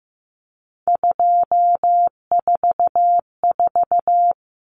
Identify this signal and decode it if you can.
Morse code, heard as a keyed tone, about 15 wpm, 700 Hz. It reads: 244
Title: Numbers - 3-Digit 15wpm